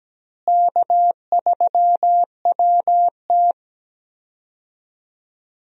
Morse code, heard as a keyed tone, about 17 wpm, 700 Hz. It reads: K3WT
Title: USA Call Signs - 4 Characters - Repeated 3x - 17wpm